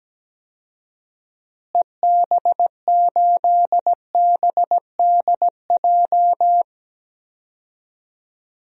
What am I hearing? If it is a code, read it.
Morse code, heard as a keyed tone, about 17 wpm, 700 Hz. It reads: EB8BDJ
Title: DXCC-100 Call Signs - 6 Characters - Repeated 3x - 17wpm